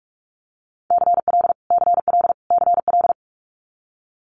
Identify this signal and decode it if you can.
Morse code, heard as a keyed tone, about 46 wpm, 700 Hz. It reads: CL CL CL
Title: Multi-letter Phonemes Repeated 3x 45wpm